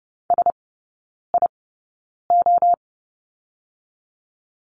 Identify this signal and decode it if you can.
Morse code, heard as a keyed tone, about 30 wpm, spaced as for 9 wpm, 700 Hz. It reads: SIO
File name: 081 Lesson 14e - Number 2 - Three Characters 30wpm